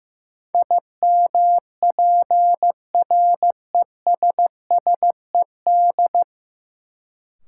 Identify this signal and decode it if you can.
Morse code, heard as a keyed tone, about 16 wpm, 700 Hz. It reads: IMPRESSED
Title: Sentences from Top 1600 Words - New Word Drill - 15wpm